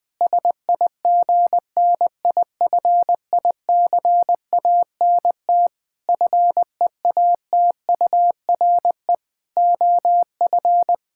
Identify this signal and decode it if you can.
Morse code, heard as a keyed tone, about 21 wpm, 700 Hz. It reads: SIGNIFICANT FEATURE OF
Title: Sentences from Top 1000 Words - Review Part 6 - 20wpm